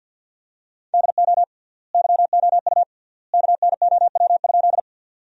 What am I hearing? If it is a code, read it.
Morse code, heard as a keyed tone, about 50 wpm, 700 Hz. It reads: DO YOU KNOW?
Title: 3x Word-Spacing - Sentences from Top 100 Words - 50wpm